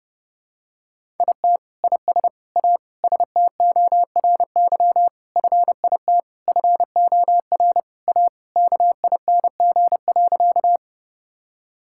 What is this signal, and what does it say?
Morse code, heard as a keyed tone, about 29 wpm, 700 Hz. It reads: IT IS A STORY FIT FOR A KING.
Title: Sentences from Top 1000 Words - Review Part 5 - 30wpm